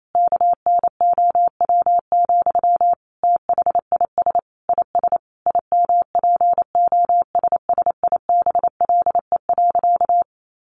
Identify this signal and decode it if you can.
Morse code, heard as a keyed tone, about 28 wpm, 700 Hz. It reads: KNOW, THIS IS IMPOSSIBLE.
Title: Sentences from Top 1100 Words 28wpm